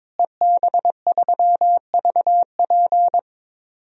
Morse code, heard as a keyed tone, about 22 wpm, 700 Hz. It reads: EB3VP
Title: DXCC-100 Call Signs - 5 Characters - Repeated 3x - 22wpm